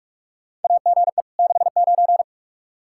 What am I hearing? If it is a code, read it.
Morse code, heard as a keyed tone, about 47 wpm, 700 Hz. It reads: AGE 69
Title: Practice QSO - Exchange 3 - More Info - 45wpm